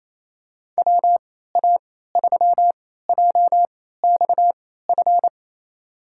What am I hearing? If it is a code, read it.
Morse code, heard as a keyed tone, about 28 wpm, spaced as for 16 wpm, 700 Hz. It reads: WA3JXF
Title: USA Call Signs - 6 Characters - 3x Character Spacing - 28wpm